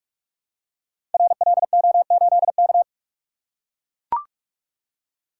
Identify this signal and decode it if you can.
Morse code, heard as a keyed tone, about 44 wpm, 700 Hz. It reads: RLO8K
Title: Code Groups - 5 Letter:Numbers - 45wpm